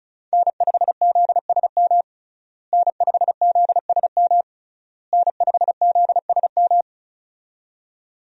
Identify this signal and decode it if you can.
Morse code, heard as a keyed tone, about 36 wpm, 700 Hz. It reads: N5ZSM N5ZSM N5ZSM
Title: USA Call Signs - 5 Characters - Repeated 3x - 35wpm